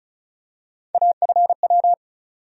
Morse code, heard as a keyed tone, about 34 wpm, 700 Hz. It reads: AFW
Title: USA Call Sign Postfixes - 3 Letters 35wpm